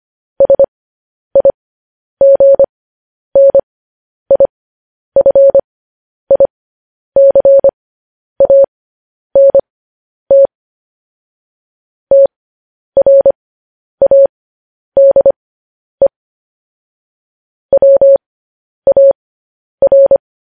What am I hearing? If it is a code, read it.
Morse code, heard as a keyed tone, about 25 wpm, spaced as for 10 wpm, 550 Hz. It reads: SIGNIFICANT TRADE WAR
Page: News Headlines in Morse Code at 10 WPM